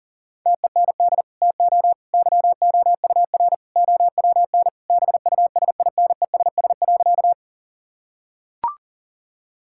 Morse code, heard as a keyed tone, about 37 wpm, 700 Hz. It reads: TEND TO YOUR OWN BUSINESS.
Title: Sentences from Top 1000 Words - Review Part 5 - 40wpm